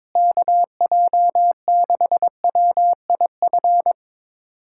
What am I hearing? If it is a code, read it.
Morse code, heard as a keyed tone, about 22 wpm, 700 Hz. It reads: KJ6WIF
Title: USA Callsign DE Callsign 22wpm